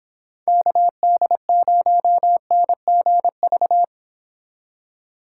Morse code, heard as a keyed tone, about 30 wpm, 700 Hz. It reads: KD0NGV